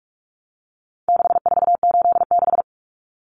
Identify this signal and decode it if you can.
Morse code, heard as a keyed tone, about 44 wpm, 700 Hz. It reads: 6486
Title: Numbers - 4-Digit 45wpm